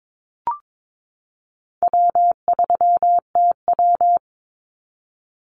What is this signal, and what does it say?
Morse code, heard as a keyed tone, about 22 wpm, 700 Hz. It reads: W3TW
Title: Easy Call Signs - 22wpm